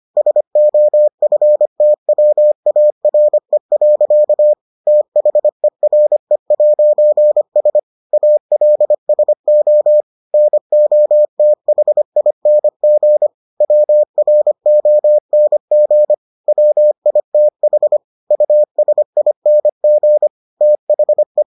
Morse code, heard as a keyed tone, about 25 wpm, 600 Hz. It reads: SOFTWARE. THERE'S ALSO NOTHING WRONG WITH USING THE